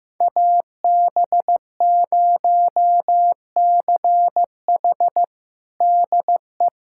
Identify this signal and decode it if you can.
Morse code, heard as a keyed tone, about 15 wpm, 700 Hz. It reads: AB0CH DE
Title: USA Callsign DE Callsign - 5 Characters - 15wpm